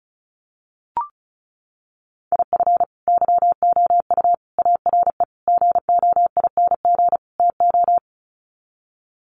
Morse code, heard as a keyed tone, about 34 wpm, 700 Hz. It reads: IF YOU ARE GOING TO
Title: Sets of 5 Words Encore2 35wpm